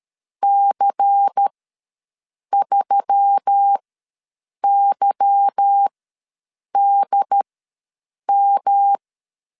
Morse code, heard as a keyed tone, about 13 wpm, spaced as for 7 wpm, 800 Hz. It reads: C3YDM
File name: LEZIONI / 8 YJ45 / 8 Y ws 12wpm 3409